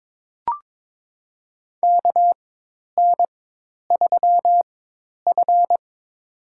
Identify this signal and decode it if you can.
Morse code, heard as a keyed tone, about 22 wpm, spaced as for 10 wpm, 700 Hz. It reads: KN3F
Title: USA Call Signs - 4 Characters - 4x Character Spacing - 22wpm